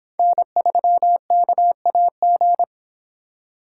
Morse code, heard as a keyed tone, about 26 wpm, 700 Hz. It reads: N3KAG